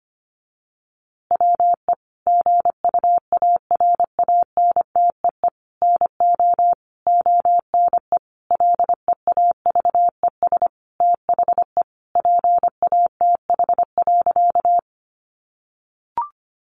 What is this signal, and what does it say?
Morse code, heard as a keyed tone, about 25 wpm, 700 Hz. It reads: WE GUARANTEE NO ONE LEAVES THE PATH.
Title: A Sound Of Thunder 25wpm